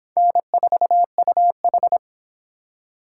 Morse code, heard as a keyed tone, about 31 wpm, 700 Hz. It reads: N4UH